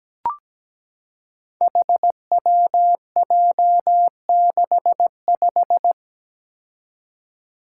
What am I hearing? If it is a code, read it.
Morse code, heard as a keyed tone, about 17 wpm, 700 Hz. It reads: HWJ65
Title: Five Challenging Letter-Number Combinations 17wpm